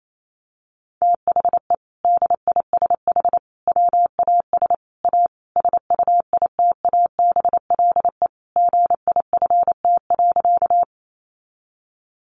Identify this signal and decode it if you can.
Morse code, heard as a keyed tone, about 28 wpm, 700 Hz. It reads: THE DISH WAS A SUITABLE GIFT.
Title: Sentences from Top 1400 Words 28wpm